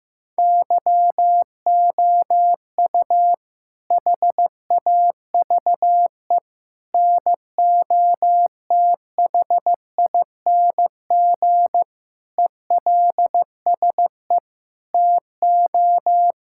Morse code, heard as a keyed tone, about 15 wpm, 700 Hz. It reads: YOU HAVE NOTHING ELSE TO
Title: Sentences from Top 1000 Words - Review Part 6 - 15wpm